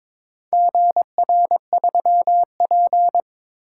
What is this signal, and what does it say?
Morse code, heard as a keyed tone, about 23 wpm, 700 Hz. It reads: GR3P
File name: practice / 22wpm / Four Letter-Number Combinations - 22wpm